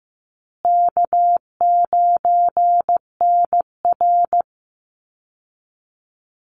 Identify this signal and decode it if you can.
Morse code, heard as a keyed tone, about 15 wpm, 700 Hz. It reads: K9NR
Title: Easy Call Signs Repeated 2x - 15wpm